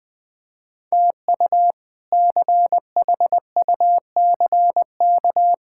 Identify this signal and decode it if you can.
Morse code, heard as a keyed tone, about 20 wpm, 700 Hz. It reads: TU CHUCK
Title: K1USN SST QSO Practice - 20wpm